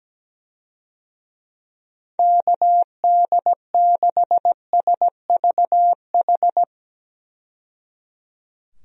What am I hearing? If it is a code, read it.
Morse code, heard as a keyed tone, about 17 wpm, 700 Hz. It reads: KD6SVH
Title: USA Call Signs - 6 Characters - 17wpm